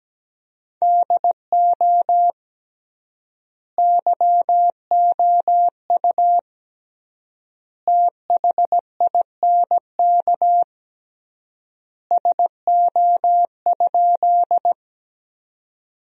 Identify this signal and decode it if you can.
Morse code, heard as a keyed tone, about 17 wpm, 700 Hz. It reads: DO YOU THINK SO?
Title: 3x Word-Spacing - Sentences from Top 200 Words - 17wpm